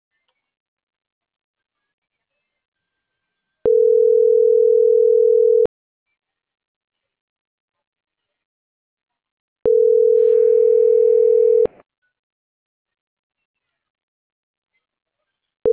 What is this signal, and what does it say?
Ringback tone (North America), 440 + 480 Hz.